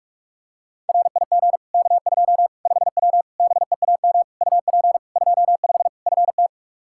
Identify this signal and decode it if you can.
Morse code, heard as a keyed tone, about 41 wpm, 700 Hz. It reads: RIG K2 5W BEAM UP 25 FT
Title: Practice QSO - Exchange 2 - Info - 45wpm